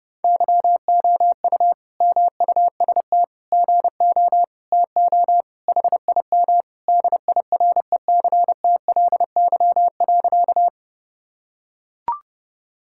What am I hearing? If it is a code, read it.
Morse code, heard as a keyed tone, about 28 wpm, 700 Hz. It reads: YOU MUST GO TO HIM DIRECTLY.
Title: Sentences from Top 1000 Words - Review Part 5 - 30wpm